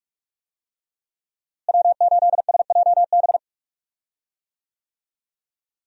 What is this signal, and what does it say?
Morse code, heard as a keyed tone, about 43 wpm, 700 Hz. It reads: W8SJB
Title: USA Call Signs - 5 Characters - Repeated 3x - 45wpm